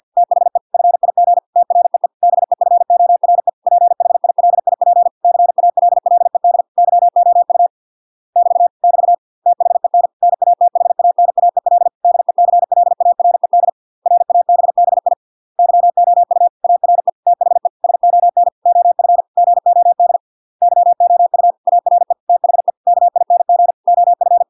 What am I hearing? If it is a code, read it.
Morse code, heard as a keyed tone, about 50 wpm, 700 Hz. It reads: THE FIG TREE BEFORE PHILIP CALLED YOU <BT> <BT> THEN NATHANAEL DECLARED RABBI YOU ARE THE SON OF GOD YOU ARE THE KING OF